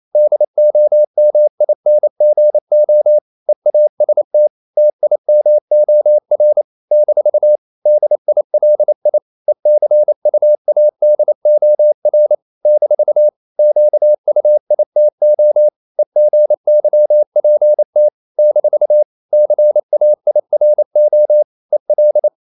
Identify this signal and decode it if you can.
Morse code, heard as a keyed tone, about 28 wpm, 600 Hz. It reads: DOMINGO EAST TIMOR - DILI ECUADOR - QUITO EGYPT - CAIRO EL
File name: CW / Practice / NationalCapitals28